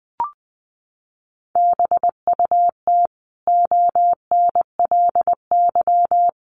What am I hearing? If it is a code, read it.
Morse code, heard as a keyed tone, about 20 wpm, 700 Hz. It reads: BUT ONLY
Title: Sentences from Top 1100 Words 20wpm